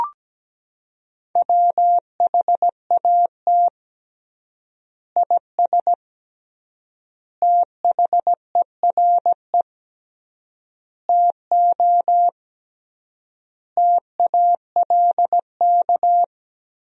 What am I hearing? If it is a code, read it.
Morse code, heard as a keyed tone, about 17 wpm, 700 Hz. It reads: WHAT IS THERE TO TALK
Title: 3x Word-Spacing - Sentences from Top 400 Words - 17wpm